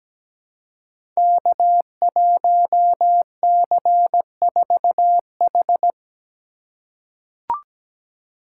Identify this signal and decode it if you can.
Morse code, heard as a keyed tone, about 17 wpm, 700 Hz. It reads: K1C4H
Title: Code Groups - 5 Letter:Numbers - 17wpm